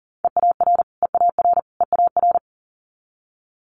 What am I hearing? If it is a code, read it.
Morse code, heard as a keyed tone, about 40 wpm, 700 Hz. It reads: EAR EAR EAR
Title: Multi-letter Phonemes Repeated 3x 40wpm